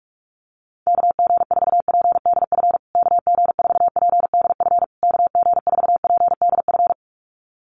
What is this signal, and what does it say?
Morse code, heard as a keyed tone, about 44 wpm, 700 Hz. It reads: KG4PDF KG4PDF KG4PDF
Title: USA Call Signs Repeated 3x 45wpm